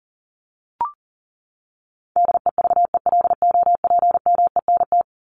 Morse code, heard as a keyed tone, about 40 wpm, 700 Hz. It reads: DEVELOPMENT
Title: Top 501-750 Words 40wpm